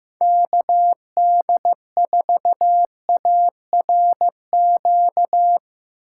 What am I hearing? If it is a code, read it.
Morse code, heard as a keyed tone, about 15 wpm, 700 Hz. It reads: KD4ARQ
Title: USA Call Signs - 6 Characters - 4x Character Spacing - 15wpm